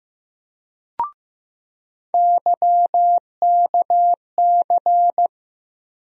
Morse code, heard as a keyed tone, about 15 wpm, 700 Hz. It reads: YKC
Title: USA Call Sign Postfixes - 3 Letters 15wpm